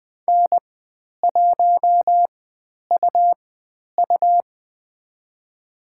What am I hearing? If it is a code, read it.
Morse code, heard as a keyed tone, about 20 wpm, spaced as for 10 wpm, 700 Hz. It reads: N1UU
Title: 109 Lesson 18g - Letter m - Call Signs 20wpm